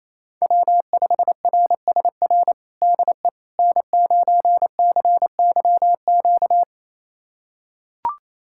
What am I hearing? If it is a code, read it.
Morse code, heard as a keyed tone, about 28 wpm, 700 Hz. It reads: W5RSR DE N9CYQ
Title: USA Callsign DE Callsign - 5 Characters - 28wpm